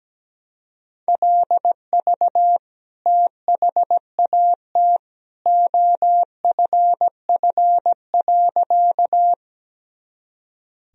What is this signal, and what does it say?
Morse code, heard as a keyed tone, about 17 wpm, 700 Hz. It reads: LV THAT OFF.
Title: Sentences from Top 400 Words and Abbreviations 17wpm